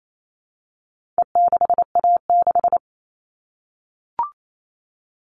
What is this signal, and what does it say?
Morse code, heard as a keyed tone, about 28 wpm, 700 Hz. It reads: E6A6
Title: All Cut Numbers - 4-Digit 28wpm